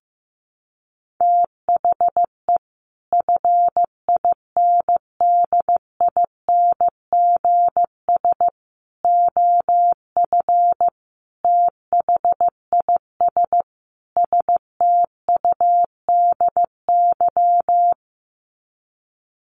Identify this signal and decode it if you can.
Morse code, heard as a keyed tone, about 14 wpm, 700 Hz. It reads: THE FINDINGS OF THIS STUDY
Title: Sets of 5 Words Encore2 15wpm